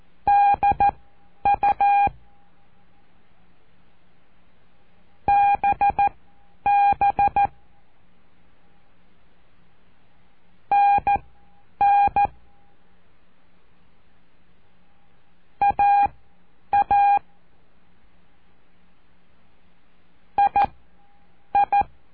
Morse code, heard as a keyed tone, about 14 wpm, spaced as for 9 wpm, 810 Hz. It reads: DU BB NN AA II
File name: cursos / CURSO DE TELEGRAFIA / Telegrafia - Trilha 05